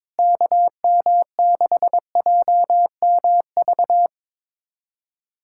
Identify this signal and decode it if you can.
Morse code, heard as a keyed tone, about 22 wpm, 700 Hz. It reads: KM6JMV
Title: USA Callsign DE Callsign - 6 Characters - 22wpm